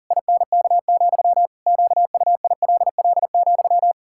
Morse code, heard as a keyed tone, about 40 wpm, 700 Hz. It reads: INK, QUILL,
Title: Pangram Challenge v2 - 40wpm